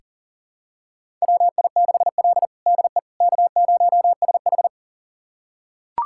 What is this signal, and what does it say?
Morse code, heard as a keyed tone, about 40 wpm, 700 Hz. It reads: WI6L DE K0SH
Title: USA Callsign DE Callsign - 4 Characters - 40wpm